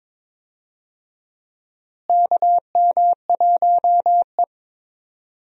Morse code, heard as a keyed tone, about 22 wpm, 700 Hz. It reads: KM1E
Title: USA Call Signs - 4 Characters - Repeated 2x - 22wpm